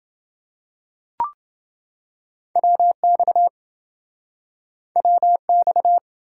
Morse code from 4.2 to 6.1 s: WX

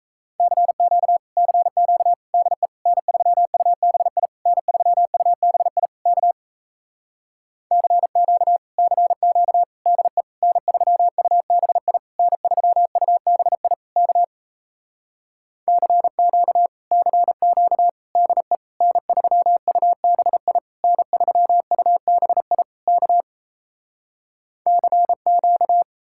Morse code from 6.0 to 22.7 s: K CQ CQ DE N3UBI N3UBI K CQ CQ DE N3UBI N3UBI

